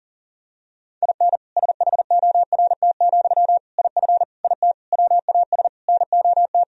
Morse code, heard as a keyed tone, about 40 wpm, 700 Hz. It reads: IN SHORT, IF IT WAS NOT
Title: Sentences from Top 1000 Words - Review Part 2 - 40wpm